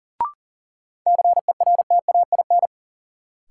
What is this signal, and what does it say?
Morse code, heard as a keyed tone, about 40 wpm, 700 Hz. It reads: CERTAIN
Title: Sentences from Top 500 Words - New Word Drill - 40wpm